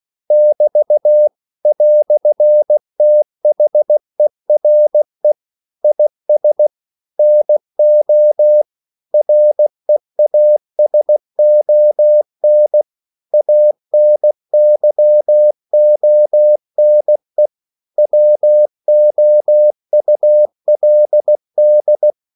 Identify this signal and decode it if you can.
Morse code, heard as a keyed tone, about 16 wpm, 600 Hz. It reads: <BT> "THERE IS NO REASON ANYONE WOULD
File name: CW / Practice / EnglishQuotations0416